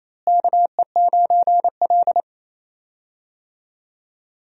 Morse code, heard as a keyed tone, about 27 wpm, 700 Hz. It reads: KE9L
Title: USA Call Signs - 4 Characters - Repeated 3x - 28wpm